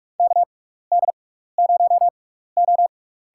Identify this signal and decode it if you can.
Morse code, heard as a keyed tone, about 47 wpm, spaced as for 16 wpm, 700 Hz. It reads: KD0O
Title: USA Call Signs - 6x Character Spacing - 45wpm